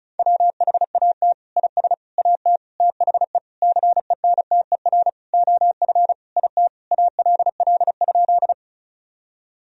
Morse code, heard as a keyed tone, about 34 wpm, 700 Hz. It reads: WHAT IS AT THE CENTER OF IT ALL?
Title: Sentences from Top 1000 Words - Review Part 2 - 35wpm